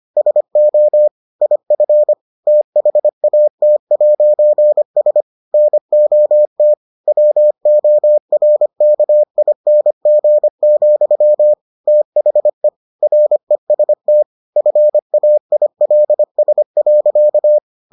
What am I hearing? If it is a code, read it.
Morse code, heard as a keyed tone, about 25 wpm, 600 Hz. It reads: SO IF THAT'S NOT WORKING, THE REST FAILS.